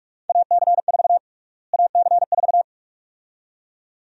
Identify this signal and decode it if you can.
Morse code, heard as a keyed tone, about 46 wpm, 700 Hz. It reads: AC4 AC4
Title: USA Call Sign Prefixes - Repeated 2x - 45wpm